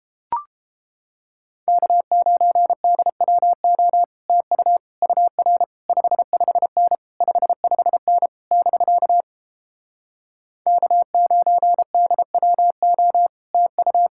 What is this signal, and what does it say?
Morse code, heard as a keyed tone, about 29 wpm, 700 Hz. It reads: K9DWO TU UR 55N 55N <BK> K9DWO TU